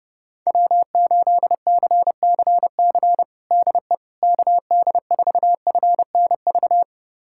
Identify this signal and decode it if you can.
Morse code, heard as a keyed tone, about 30 wpm, 700 Hz. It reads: W8CCC DE KD4FNV
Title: USA Callsign DE Callsign 30wpm